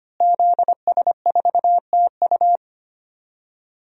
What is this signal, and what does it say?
Morse code, heard as a keyed tone, about 25 wpm, 700 Hz. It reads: ZS4TU
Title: DXCC-100 Call Signs - 5 Characters - Repeated 3x - 25wpm